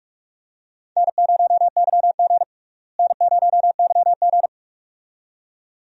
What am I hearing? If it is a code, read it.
Morse code, heard as a keyed tone, about 46 wpm, 700 Hz. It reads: N0YG N0YG
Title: USA Call Signs - 4 Characters - Repeated 2x - 45wpm